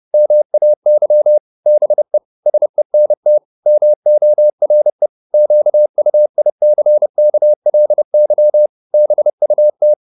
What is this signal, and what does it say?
Morse code, heard as a keyed tone, about 30 wpm, 600 Hz. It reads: MAY BE SENT MORE QUICKLY BUT